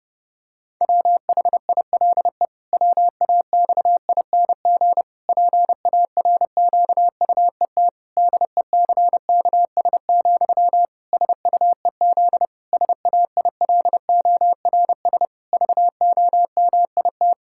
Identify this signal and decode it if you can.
Morse code, heard as a keyed tone, about 30 wpm, 700 Hz. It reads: WHILE WAXING PARQUET DECKS, SUEZ SAILORS VOMIT
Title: Pangram Challenge v2 - 30wpm